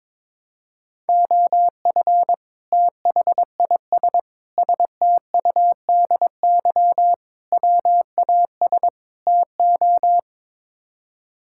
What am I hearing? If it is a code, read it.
Morse code, heard as a keyed tone, about 21 wpm, 700 Hz. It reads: OF THIS STUDY WAS TO
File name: practice / 22wpm / Build A Phrase - 5-Word Phrases - 22wpm